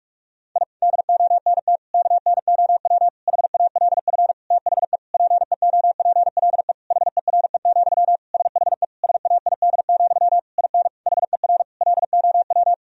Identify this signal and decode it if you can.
Morse code, heard as a keyed tone, about 46 wpm, 700 Hz. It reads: I DONT KNOW HALF THE PEOPLE HERE, SHE SAID, IN HER LOW
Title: Sentences from Top 1000 Words - Review Part 1 - 45wpm